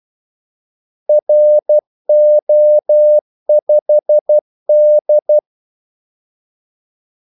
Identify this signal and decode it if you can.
Morse code, heard as a keyed tone, about 12 wpm, 600 Hz. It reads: RO5D
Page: 2 ekstra ordmellemrum mellem kaldesignalerne